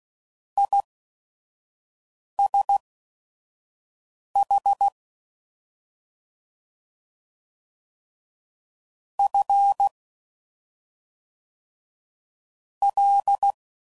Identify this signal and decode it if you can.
Morse code, heard as a keyed tone, about 16 wpm, spaced as for 5 wpm, 800 Hz. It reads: ISH F L